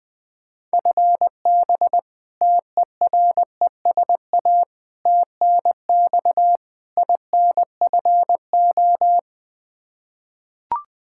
Morse code, heard as a keyed tone, about 21 wpm, 700 Hz. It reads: FB TERESA TNX INFO
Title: Practice QSO - Exchange 3 - More Info - 20wpm